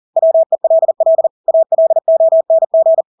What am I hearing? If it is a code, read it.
Morse code, heard as a keyed tone, about 40 wpm, 650 Hz. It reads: WELL ALONG
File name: Engelske ord 40-40 wpm